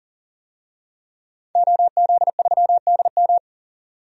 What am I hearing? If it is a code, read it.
Morse code, heard as a keyed tone, about 40 wpm, 700 Hz. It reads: OZ3DM
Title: DXCC-100 Call Signs - 5 Characters - 40wpm